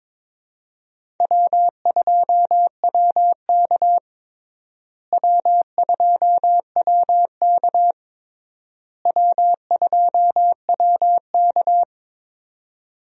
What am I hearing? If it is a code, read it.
Morse code, heard as a keyed tone, about 22 wpm, 700 Hz. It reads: W2WK W2WK W2WK
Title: USA Call Signs - 4 Characters - Repeated 3x - 22wpm